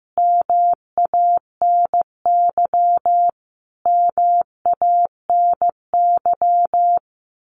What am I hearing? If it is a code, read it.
Morse code, heard as a keyed tone, about 15 wpm, 700 Hz. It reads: MANY MANY
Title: Top 200 Words Repeated 15wpm